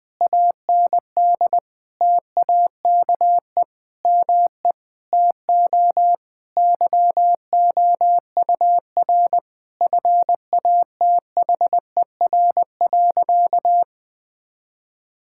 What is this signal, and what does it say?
Morse code, heard as a keyed tone, about 20 wpm, 700 Hz. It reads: AND TAKE ME TO YOUR FATHER.
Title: Sentences from Top 1000 Words - Review Part 6 - 20wpm